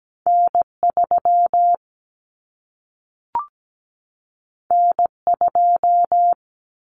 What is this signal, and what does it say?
Morse code, heard as a keyed tone, about 17 wpm, 700 Hz. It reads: N3 N2
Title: Common Cut Numbers - 2-Digit 17wpm